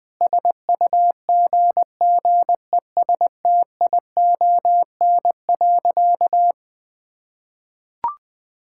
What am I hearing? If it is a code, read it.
Morse code, heard as a keyed tone, about 20 wpm, 700 Hz. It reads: SUGGESTION.
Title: Sentences from Top 1600 Words 20wpm